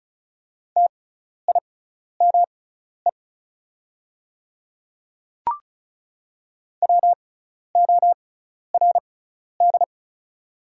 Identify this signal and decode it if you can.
Morse code, heard as a keyed tone, about 36 wpm, spaced as for 12 wpm, 700 Hz. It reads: TIME WORD
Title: 6x Character Spacing - Top 100 Words - 35wpm